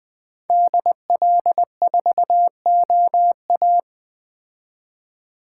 Morse code, heard as a keyed tone, about 21 wpm, 700 Hz. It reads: DL4OA